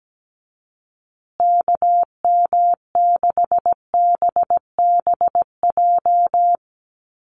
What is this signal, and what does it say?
Morse code, heard as a keyed tone, about 17 wpm, 700 Hz. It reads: KM6BBJ
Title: USA Call Signs Repeated 3x 17wpm